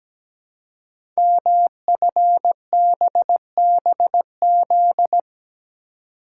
Morse code, heard as a keyed tone, about 17 wpm, 700 Hz. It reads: MFBBZ
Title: Code Groups - 5 Letters - 17wpm